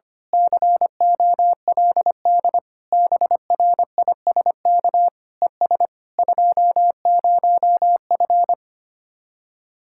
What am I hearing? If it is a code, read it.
Morse code, heard as a keyed tone, about 24 wpm, 700 Hz. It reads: COLD BRISK ES 20F
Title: Speed-Racing Common QSO Phrases - 25wpm